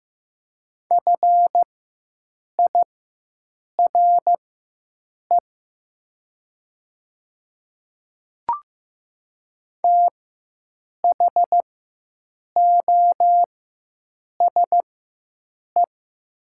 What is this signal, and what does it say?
Morse code, heard as a keyed tone, about 15 wpm, spaced as for 7 wpm, 700 Hz. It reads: FIRE THOSE
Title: 4x Character Spacing - Top 500 Words - 15wpm